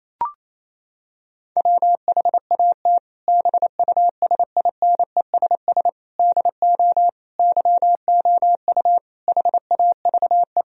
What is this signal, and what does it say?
Morse code, heard as a keyed tone, about 28 wpm, 700 Hz. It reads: WHAT BUSINESS DO YOU HAVE
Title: Sentences from Top 1000 Words - Review Part 3 - 28wpm